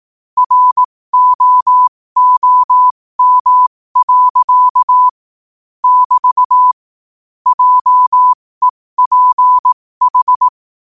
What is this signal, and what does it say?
Morse code, heard as a keyed tone, about 18 wpm, spaced as for 15 wpm, 1000 Hz. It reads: ROOM. <BT> JEPH
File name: qotdfiles / 15wpm